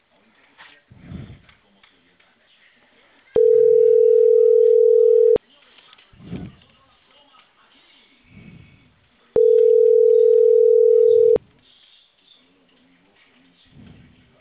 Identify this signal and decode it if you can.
Ringback tone (North America), 440 + 480 Hz, with speech.
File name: CENTRAL CAEA / out agcaea02 - .21365